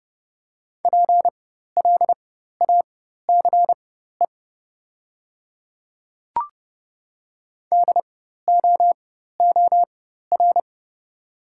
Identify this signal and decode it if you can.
Morse code, heard as a keyed tone, about 30 wpm, spaced as for 14 wpm, 700 Hz. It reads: PLACE DOOR
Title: 4x Character Spacing - Top 300 Words - 30wpm